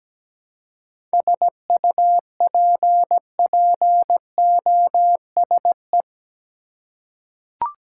Morse code, heard as a keyed tone, about 17 wpm, 700 Hz. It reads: SUPPOSE
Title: Seven Letter Words - 17wpm